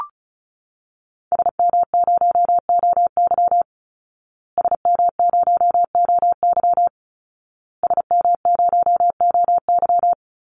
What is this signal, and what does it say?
Morse code, heard as a keyed tone, about 34 wpm, 700 Hz. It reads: SM0OY SM0OY SM0OY
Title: Call Signs Repeated 2x - 35wpm